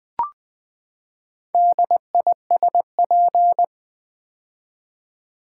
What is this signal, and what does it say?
Morse code, heard as a keyed tone, about 20 wpm, 700 Hz. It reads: DISP
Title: Build A Word - 4 Letters At A Time - 8-Letter Words - 20wpm